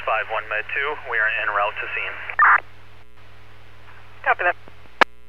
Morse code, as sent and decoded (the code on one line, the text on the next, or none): none